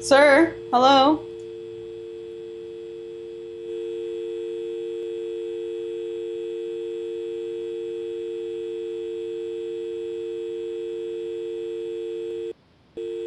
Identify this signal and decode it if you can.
Dial tone (UK), 350 + 450 Hz, with speech.